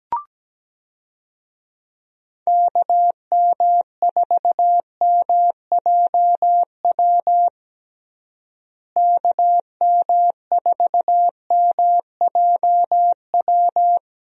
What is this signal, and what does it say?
Morse code, heard as a keyed tone, about 17 wpm, 700 Hz. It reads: KM4MJW KM4MJW
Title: USA Call Signs - 6 Characters - Repeated 3x - 17wpm